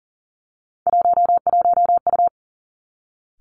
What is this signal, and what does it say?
Morse code, heard as a keyed tone, about 38 wpm, 700 Hz. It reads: 11U
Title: Three Challenging Letter-Number Combinations 40wpm